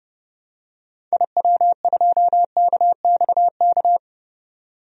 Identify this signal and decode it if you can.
Morse code, heard as a keyed tone, about 30 wpm, 700 Hz. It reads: IW2KXK
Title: DXCC-100 Call Signs - 6 Characters - 30wpm